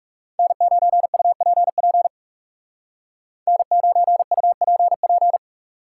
Morse code, heard as a keyed tone, about 37 wpm, 700 Hz. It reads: N9UPP N9UPP
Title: Speed Racing - USA Call Signs 30wpm